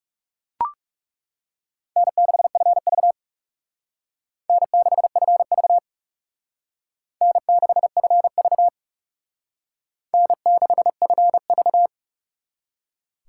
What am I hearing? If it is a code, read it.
Morse code, heard as a keyed tone, about 37 wpm, 700 Hz. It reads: N6FV N6FV N6FV N6FV
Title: Speed Racing - USA 1x2 Call Signs 30wpm